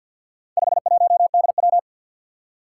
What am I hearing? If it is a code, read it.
Morse code, heard as a keyed tone, about 46 wpm, 700 Hz. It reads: 51DW
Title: Four Challenging Letter-Number Combinations 50wpm